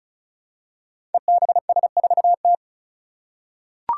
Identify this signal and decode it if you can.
Morse code, heard as a keyed tone, about 36 wpm, 700 Hz. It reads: EBS4T
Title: Five Challenging Letter-Number Combinations 35wpm